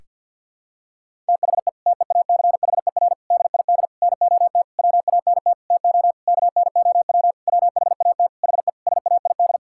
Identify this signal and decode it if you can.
Morse code, heard as a keyed tone, about 50 wpm, 700 Hz. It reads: THE TEACHER DID NOT WANT TO KNOW WHAT HE SAID
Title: Sentences from Top 1000 Words - Review Part 6 - 50wpm